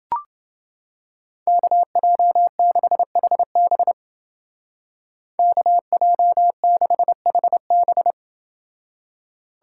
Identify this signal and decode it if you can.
Morse code, heard as a keyed tone, about 29 wpm, 700 Hz. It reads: KJ6HB KJ6HB